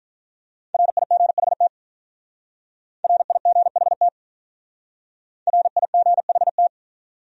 Morse code, heard as a keyed tone, about 48 wpm, 700 Hz. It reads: RIGHT RIGHT RIGHT